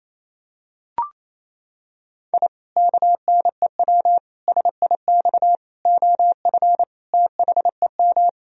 Morse code, heard as a keyed tone, about 28 wpm, 700 Hz. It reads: I KNEW SIX OF THEM
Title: Sentences from Top 1000 Words - Review Part 2 - 28wpm